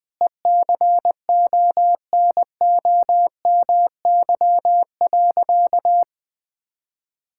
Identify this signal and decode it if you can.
Morse code, heard as a keyed tone, about 20 wpm, 700 Hz. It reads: ECONOMY.
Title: Sentences from Top 1600 Words 20wpm